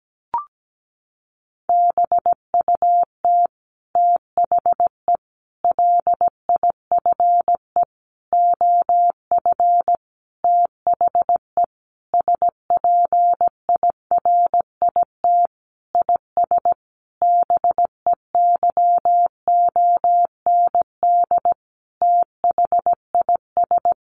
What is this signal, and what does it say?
Morse code, heard as a keyed tone, about 17 wpm, 700 Hz. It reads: BUT THE LIFE OF THE SPIRIT IS BEYOND THIS
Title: Sentences from Top 1500 Words 17wpm